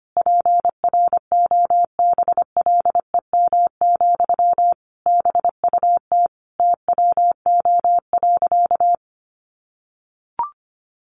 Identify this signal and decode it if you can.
Morse code, heard as a keyed tone, about 25 wpm, 700 Hz. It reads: PROBLEM, BUT TWO.
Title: Sentences from Top 1400 Words 25wpm